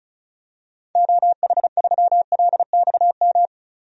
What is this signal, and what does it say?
Morse code, heard as a keyed tone, about 36 wpm, 690 Hz. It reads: OH3LXM
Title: DXCC-100 Call Signs - 6 Characters - 35wpm